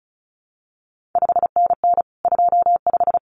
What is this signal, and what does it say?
Morse code, heard as a keyed tone, about 36 wpm, 700 Hz. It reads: 5NN 25
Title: CQ WW DX Contest Exchange - 35wpm